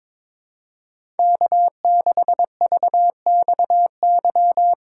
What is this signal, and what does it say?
Morse code, heard as a keyed tone, about 22 wpm, 700 Hz. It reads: K6VXY
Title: USA Call Signs - 5 Characters - 3x Character Spacing - 22wpm